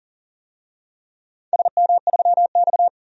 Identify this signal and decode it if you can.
Morse code, heard as a keyed tone, about 40 wpm, 700 Hz. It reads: SM3X
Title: DXCC-100 Call Signs - 4 Characters - 40wpm